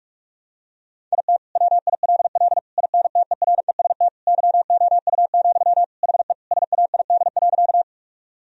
Morse code, heard as a keyed tone, about 46 wpm, 700 Hz. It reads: IT WILL INTEREST YOU, HE SAID.
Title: Sentences from Top 1000 Words - Review Part 2 - 45wpm